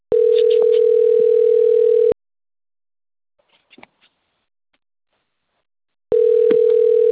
Ringback tone (North America), 440 + 480 Hz.
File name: out- - - - - .11406